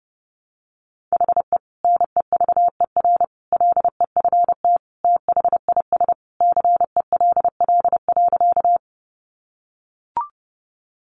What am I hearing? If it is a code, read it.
Morse code, heard as a keyed tone, about 30 wpm, 700 Hz. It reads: HE NEVER LEFT THIS CELL.
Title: Sentences from Top 1100 Words 30wpm